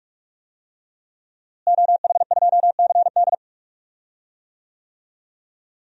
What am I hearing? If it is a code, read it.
Morse code, heard as a keyed tone, about 46 wpm, 700 Hz. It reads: OH2CD
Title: DXCC-100 Call Signs - 5 Characters - Repeated 2x - 45wpm